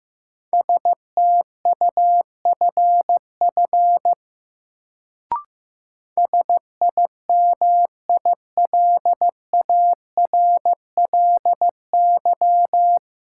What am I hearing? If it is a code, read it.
Morse code, heard as a keyed tone, about 15 wpm, 700 Hz. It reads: STUFF SIMILARLY
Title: Sentences from Top 1400 Words - New Word Drill - 15wpm